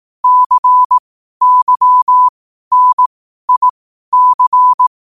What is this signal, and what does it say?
Morse code, heard as a keyed tone, about 18 wpm, spaced as for 12 wpm, 1000 Hz. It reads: CYNIC
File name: qotdfiles / 12wpm